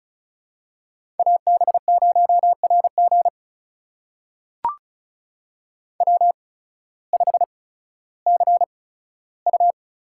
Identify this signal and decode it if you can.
Morse code, heard as a keyed tone, about 36 wpm, spaced as for 16 wpm, 700 Hz. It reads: AB0RG W5CU
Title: USA Call Signs - 5 Characters - 8x Character Spacing - 35wpm